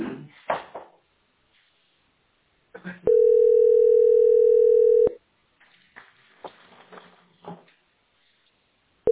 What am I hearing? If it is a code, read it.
Ringback tone (North America), 440 + 480 Hz.
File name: d42a6142-74fe-4f3c-8c65